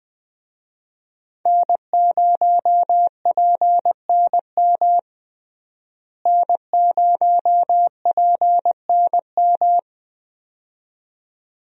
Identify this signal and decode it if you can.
Morse code, heard as a keyed tone, about 20 wpm, 700 Hz. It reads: N0PNM N0PNM
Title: USA Call Signs - 5 Characters - Repeated 2x - 20wpm